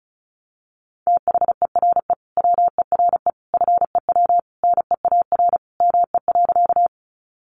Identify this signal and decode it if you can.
Morse code, heard as a keyed tone, about 34 wpm, 700 Hz. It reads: THERE WERE FEW NEAR ME.
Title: Sentences from Top 300 Words 35wpm